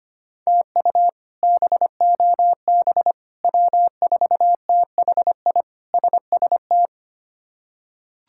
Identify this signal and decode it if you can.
Morse code, heard as a keyed tone, about 25 wpm, 700 Hz. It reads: TU BOB W4THI SST
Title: K1USN SST QSO Practice - 25wpm